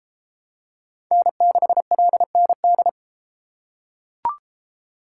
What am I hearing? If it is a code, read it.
Morse code, heard as a keyed tone, about 31 wpm, 700 Hz. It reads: N6LND